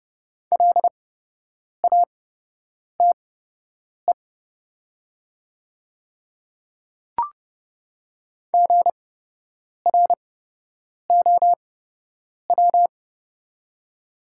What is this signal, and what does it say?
Morse code, heard as a keyed tone, about 30 wpm, spaced as for 8 wpm, 700 Hz. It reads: LATE GROW
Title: 8x Character Spacing - Top 300 Words - 30wpm